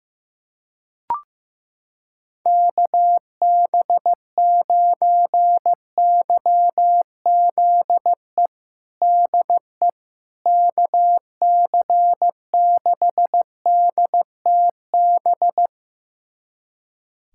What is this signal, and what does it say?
Morse code, heard as a keyed tone, about 16 wpm, 700 Hz. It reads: KB9YZE DE KC6DTB
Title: USA Callsign DE Callsign - 6 Characters - 15wpm